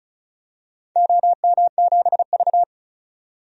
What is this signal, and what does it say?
Morse code, heard as a keyed tone, about 36 wpm, 690 Hz. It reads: OM7V
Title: DXCC-100 Call Signs - 4 Characters - 35wpm